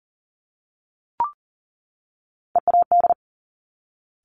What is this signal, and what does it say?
Morse code, heard as a keyed tone, about 38 wpm, 690 Hz. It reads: EAD
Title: Top Three Letter Combinations in Words 40wpm